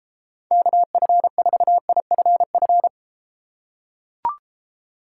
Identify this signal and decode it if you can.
Morse code, heard as a keyed tone, about 33 wpm, 700 Hz. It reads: KF4IFF